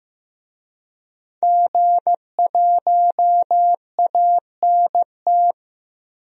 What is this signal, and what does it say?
Morse code, heard as a keyed tone, about 15 wpm, 700 Hz. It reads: G1ANT
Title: DXCC-100 Call Signs - 5 Characters - 15wpm